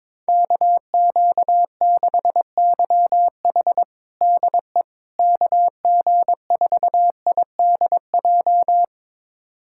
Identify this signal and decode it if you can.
Morse code, heard as a keyed tone, about 22 wpm, 700 Hz. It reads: KQ6YH DE KG4IDJ
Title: USA Callsign DE Callsign 22wpm